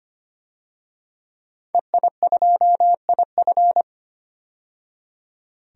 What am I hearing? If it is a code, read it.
Morse code, heard as a keyed tone, about 25 wpm, 700 Hz. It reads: EI2IF
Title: DXCC-100 Call Signs - 5 Characters - 25wpm